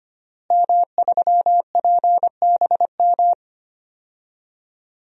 Morse code, heard as a keyed tone, about 25 wpm, 700 Hz. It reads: M3PBM
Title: DXCC-100 Call Signs - 5 Characters - Repeated 2x - 25wpm